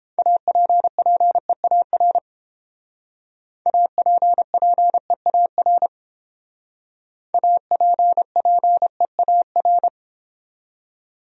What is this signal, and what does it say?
Morse code, heard as a keyed tone, about 30 wpm, 700 Hz. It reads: APPEAR APPEAR APPEAR